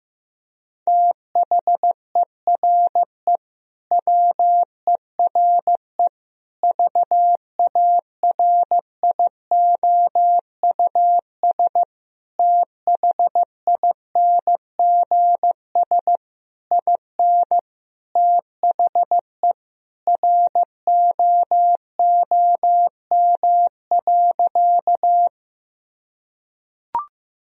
Morse code, heard as a keyed tone, about 15 wpm, 700 Hz. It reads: THERE WERE VARIOUS THINGS IN THE ROOM.
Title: Sentences from Top 1000 Words - Review Part 4 - 15wpm